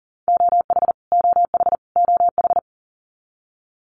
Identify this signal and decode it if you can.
Morse code, heard as a keyed tone, about 40 wpm, 700 Hz. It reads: OH OH OH
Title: Top 500 Words Repeated 40wpm